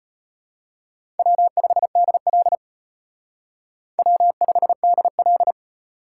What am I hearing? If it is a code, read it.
Morse code, heard as a keyed tone, about 32 wpm, 700 Hz. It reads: W5DL W5DL